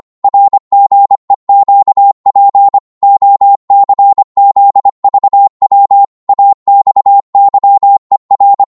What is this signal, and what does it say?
Morse code, heard as a keyed tone, about 25 wpm, 800 Hz. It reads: RGEQP OCZVW AXYER
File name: Morse / ltr-25wpm